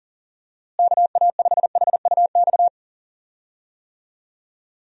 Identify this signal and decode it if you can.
Morse code, heard as a keyed tone, about 40 wpm, 700 Hz. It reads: KA5HUX
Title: USA Call Signs - 6 Characters - Repeated 2x - 40wpm